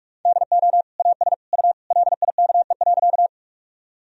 Morse code, heard as a keyed tone, about 46 wpm, 700 Hz. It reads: DO AS U LIKE.
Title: Sentences from Top 100 Words and Abbreviations 45wpm